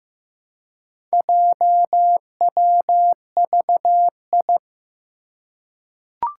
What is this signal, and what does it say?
Morse code, heard as a keyed tone, about 15 wpm, 700 Hz. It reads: JWVI
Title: Four Challenging Letter-Number Combinations 15wpm